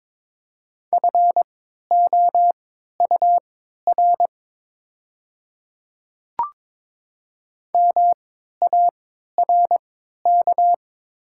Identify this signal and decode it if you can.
Morse code, heard as a keyed tone, about 22 wpm, spaced as for 12 wpm, 700 Hz. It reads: FOUR MARK
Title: 3x Character Spacing - Top 300 Words - 22wpm